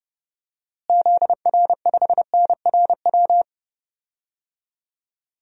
Morse code, heard as a keyed tone, about 32 wpm, 700 Hz. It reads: ZR5NRW
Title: DXCC-100 Call Signs - 6 Characters - Repeated 3x - 30wpm